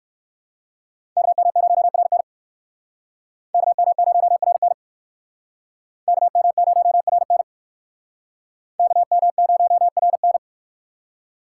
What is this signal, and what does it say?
Morse code, heard as a keyed tone, about 57 wpm, 700 Hz. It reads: KM0RN KM0RN KM0RN KM0RN